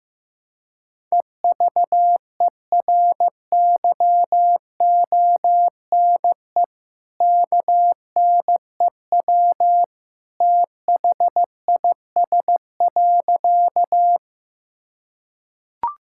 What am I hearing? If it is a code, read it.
Morse code, heard as a keyed tone, about 15 wpm, 700 Hz. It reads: EVERYONE KNEW THIS.
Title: Sentences from Top 1000 Words - Review Part 5 - 15wpm